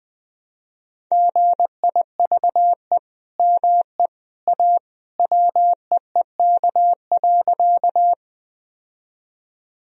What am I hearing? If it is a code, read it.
Morse code, heard as a keyed tone, about 20 wpm, 700 Hz. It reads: GIVE ME A WEEK.
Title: Sentences from Top 1000 Words - Review Part 2 - 20wpm